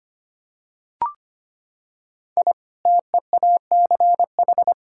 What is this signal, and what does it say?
Morse code, heard as a keyed tone, about 25 wpm, 700 Hz. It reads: I TEACH
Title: Sentences from Top 1000 Words - Review Part 6 - 25wpm